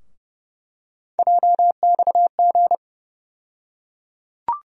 Morse code, heard as a keyed tone, about 32 wpm, 700 Hz. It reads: JXG
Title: USA Call Sign Postfixes - 3 Letters 30wpm